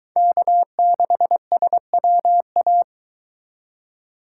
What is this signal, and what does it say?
Morse code, heard as a keyed tone, about 23 wpm, 700 Hz. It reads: K6SWA